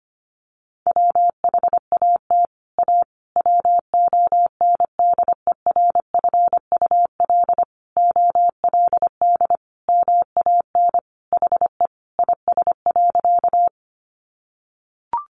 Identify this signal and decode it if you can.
Morse code, heard as a keyed tone, about 25 wpm, 700 Hz. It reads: WHAT A WONDERFUL OLD MAN HE IS.
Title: Sentences from Top 1100 Words 25wpm